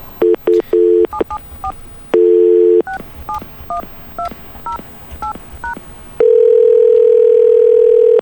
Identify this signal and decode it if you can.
DTMF, heard as telephone touch tones, tones of about 120 ms. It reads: 774 671208#